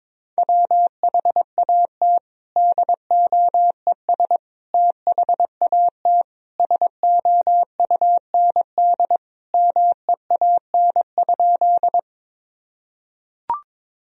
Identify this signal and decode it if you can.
Morse code, heard as a keyed tone, about 22 wpm, 700 Hz. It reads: WHAT DOES THAT SOUND MEAN?
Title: Sentences from Top 1000 Words - Review Part 1 - 22wpm